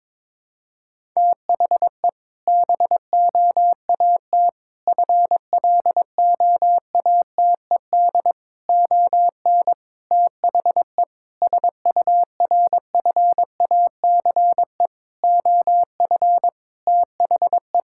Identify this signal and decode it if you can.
Morse code, heard as a keyed tone, about 22 wpm, 700 Hz. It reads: THE BOAT FLOATED ON THE SURFACE OF THE
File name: practice / 22wpm / Sentences with 4-Word Phrases - 22wpm